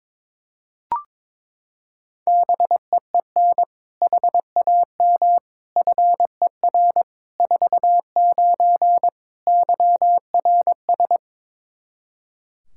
Morse code, heard as a keyed tone, about 22 wpm, 700 Hz. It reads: BEEN HAM FER 49 YRS
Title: K1USN SST QSO Practice 22wpm